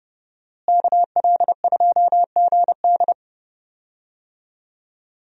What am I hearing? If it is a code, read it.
Morse code, heard as a keyed tone, about 31 wpm, 700 Hz. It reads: KL2GD
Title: USA Call Signs - 5 Characters - Repeated 2x - 30wpm